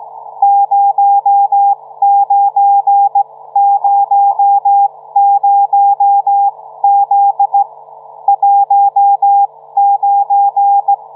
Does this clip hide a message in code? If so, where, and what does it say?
Yes: Morse code from 0.4 to 10.9 s, heard as a keyed tone, about 16 wpm, spaced as for 14 wpm, 800 Hz. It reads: 0900Z 19